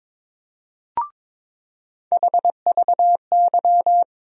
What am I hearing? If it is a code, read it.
Morse code, heard as a keyed tone, about 22 wpm, 700 Hz. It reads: HVY
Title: USA Call Sign Postfixes - 3 Letters 22wpm